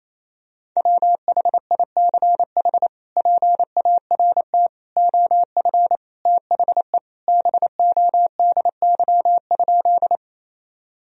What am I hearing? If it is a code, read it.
Morse code, heard as a keyed tone, about 27 wpm, 700 Hz. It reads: WHICH PART OF THE BODY?
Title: Sentences from Top 1000 Words - Review Part 2 - 28wpm